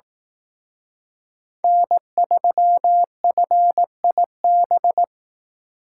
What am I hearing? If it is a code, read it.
Morse code, heard as a keyed tone, about 18 wpm, 700 Hz. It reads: N3FIB
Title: Speed Racing - USA 1x3 Call Signs 15wpm